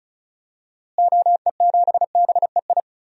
Morse code, heard as a keyed tone, about 36 wpm, 700 Hz. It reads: OE7BEI
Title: DXCC-100 Call Signs - 6 Characters - Repeated 2x - 35wpm